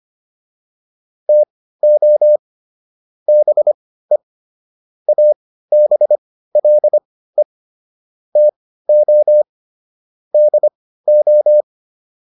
Morse code, heard as a keyed tone, about 25 wpm, spaced as for 15 wpm, 600 Hz. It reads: TO BE ABLE TO DO